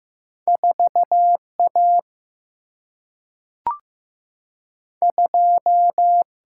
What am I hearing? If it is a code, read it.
Morse code, heard as a keyed tone, about 15 wpm, 700 Hz. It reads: 4 A 2
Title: Single-Letter Single-Number 15wpm